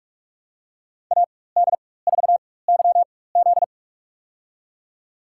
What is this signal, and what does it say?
Morse code, heard as a keyed tone, about 45 wpm, spaced as for 21 wpm, 700 Hz. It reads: AD4YZ
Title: USA Call Signs - 5 Characters - 4x Character Spacing - 45wpm